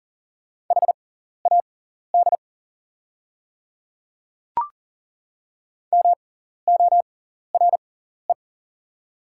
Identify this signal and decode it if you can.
Morse code, heard as a keyed tone, about 40 wpm, spaced as for 14 wpm, 700 Hz. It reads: HAD MORE
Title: 6x Character Spacing - Top 100 Words - 40wpm